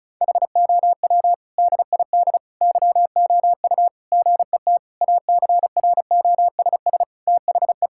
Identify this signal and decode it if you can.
Morse code, heard as a keyed tone, about 34 wpm, 700 Hz. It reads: HOW DID YOU GET ACROSS THE
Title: Sentences from Top 1000 Words - Review Part 5 - 35wpm